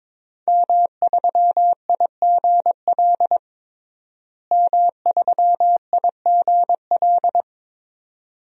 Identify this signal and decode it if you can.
Morse code, heard as a keyed tone, about 22 wpm, 700 Hz. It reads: M3IGL M3IGL
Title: DXCC-100 Call Signs - 5 Characters - Repeated 2x - 22wpm